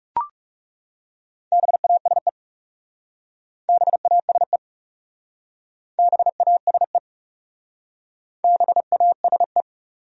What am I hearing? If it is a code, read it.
Morse code, heard as a keyed tone, about 36 wpm, 700 Hz. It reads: BASE BASE BASE BASE
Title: Speed-Racing Top 500 Words 30wpm